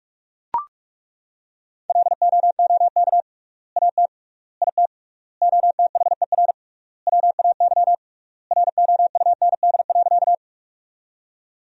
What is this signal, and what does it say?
Morse code, heard as a keyed tone, about 41 wpm, 700 Hz. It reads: LOOK AT IT OTHER WAY ROUND.
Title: 3x Word-Spacing - Sentences from Top 200 Words - 45wpm